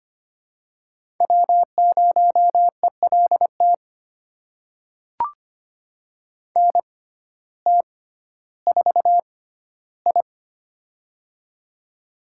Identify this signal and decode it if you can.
Morse code, heard as a keyed tone, about 25 wpm, spaced as for 14 wpm, 700 Hz. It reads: W0ELT NT4I
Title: USA Call Signs - 6x Character Spacing - 25wpm